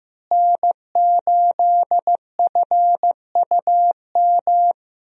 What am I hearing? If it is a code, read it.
Morse code, heard as a keyed tone, about 16 wpm, 700 Hz. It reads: N8FUM
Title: USA Call Signs - 5 Characters - Repeated 2x - 15wpm